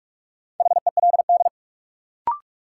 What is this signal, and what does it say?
Morse code, heard as a keyed tone, about 44 wpm, 700 Hz. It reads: HELD
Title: Sentences from Top 1600 Words - New Word Drill - 45wpm